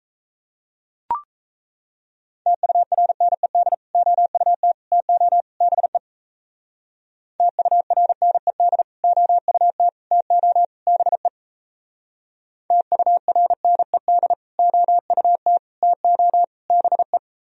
Morse code, heard as a keyed tone, about 34 wpm, 700 Hz. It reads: TURNED OUT TO BE TURNED OUT TO BE TURNED OUT TO BE
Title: Speed-Racing - Sets of 4 Words 28wpm